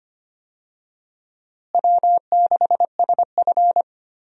Morse code, heard as a keyed tone, about 25 wpm, 700 Hz. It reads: W6SF
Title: USA Call Signs - 4 Characters - Repeated 3x - 25wpm